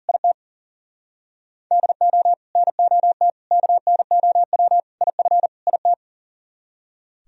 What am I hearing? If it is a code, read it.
Morse code, heard as a keyed tone, about 38 wpm, 700 Hz. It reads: IT DO NOT KNOW IF IT
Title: Speed-Racing - Sets of 5 Words 40wpm